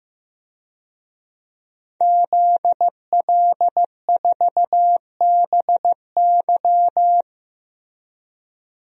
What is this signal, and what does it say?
Morse code, heard as a keyed tone, about 15 wpm, 700 Hz. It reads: ZL4BY
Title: DXCC-100 Call Signs - 5 Characters - Repeated 2x - 15wpm